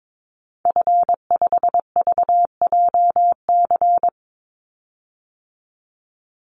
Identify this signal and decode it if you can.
Morse code, heard as a keyed tone, about 22 wpm, 700 Hz. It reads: F5VJC
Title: Call Signs - 22wpm